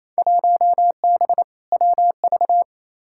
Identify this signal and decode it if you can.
Morse code, heard as a keyed tone, about 28 wpm, 700 Hz. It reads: 1B WV
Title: Field Day Exchanges Encore - 28wpm